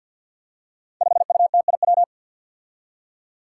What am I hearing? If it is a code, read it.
Morse code, heard as a keyed tone, about 53 wpm, 700 Hz. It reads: 5UTIW
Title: Five Challenging Letter-Number Combinations 50wpm